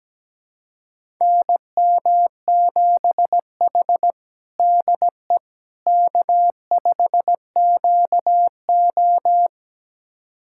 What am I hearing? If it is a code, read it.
Morse code, heard as a keyed tone, about 17 wpm, 700 Hz. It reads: NM7H DE K5QO
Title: USA Callsign DE Callsign - 4 Characters - 17wpm